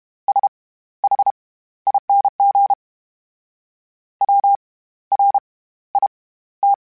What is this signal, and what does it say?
Morse code, heard as a keyed tone, about 32 wpm, spaced as for 15 wpm, 800 Hz. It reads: SHING WRIT
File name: Special INGs / Special INGs-mp3s / SlowToFaster ing @30